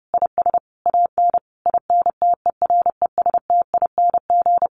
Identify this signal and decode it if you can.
Morse code, heard as a keyed tone, about 28 wpm, 700 Hz. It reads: IS AN INTERESTING
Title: Easy Phrases 30wpm